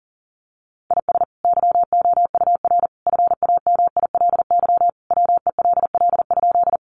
Morse code, heard as a keyed tone, about 40 wpm, 700 Hz. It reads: IS YOUR FAMILY WELL?
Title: Sentences from Top 400 Words 40wpm